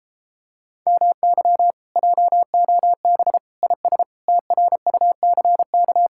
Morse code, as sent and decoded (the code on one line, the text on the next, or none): -- -.-- / .--- --- -... / .. ... / - .-. ..- -.-. -.-
MY JOB IS TRUCK